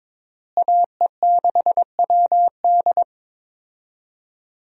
Morse code, heard as a keyed tone, about 22 wpm, 700 Hz. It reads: AE6WD
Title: USA Call Signs - 5 Characters - Repeated 3x - 22wpm